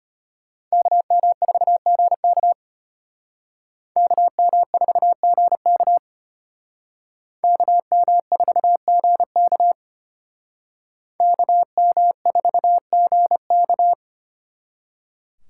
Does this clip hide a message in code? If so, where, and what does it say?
Yes: Morse code from 0.7 to 13.9 s, heard as a keyed tone, about 32 wpm, 700 Hz. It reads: KM4GK KM4GK KM4GK KM4GK